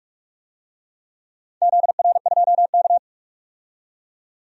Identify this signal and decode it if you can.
Morse code, heard as a keyed tone, about 42 wpm, 700 Hz. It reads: ZR2K
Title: DXCC-100 Call Signs - 4 Characters - 45wpm